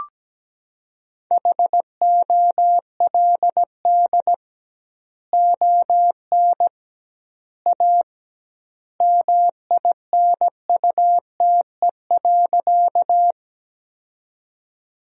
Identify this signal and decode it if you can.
Morse code, heard as a keyed tone, about 17 wpm, 700 Hz. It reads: HOLD ON A MINUTE.
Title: 2x Word-Spacing - Sentences from Top 500 Words - 17wpm